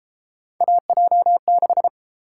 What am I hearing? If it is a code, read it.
Morse code, heard as a keyed tone, about 33 wpm, 700 Hz. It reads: AJ6